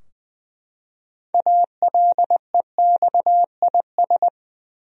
Morse code, heard as a keyed tone, about 20 wpm, 700 Hz. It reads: ALEXIS
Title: USA Common Names - 20wpm